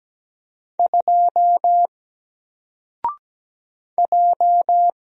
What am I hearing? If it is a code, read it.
Morse code, heard as a keyed tone, about 17 wpm, 700 Hz. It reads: 2 J
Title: Single Letter-Number Alt 17wpm